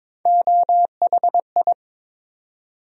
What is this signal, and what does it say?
Morse code, heard as a keyed tone, about 22 wpm, 700 Hz. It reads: OHI
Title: USA Call Sign Postfixes - 3 Letters 22wpm